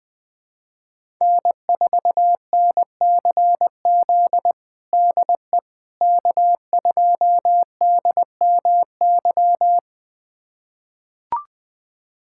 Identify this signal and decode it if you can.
Morse code, heard as a keyed tone, about 21 wpm, 700 Hz. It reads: N4NCZ DE K2DMY
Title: USA Callsign DE Callsign - 5 Characters - 20wpm